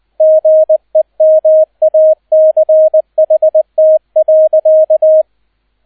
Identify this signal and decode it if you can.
Morse code, heard as a keyed tone, about 20 wpm, 630 Hz, with radio noise. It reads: GEMACHT.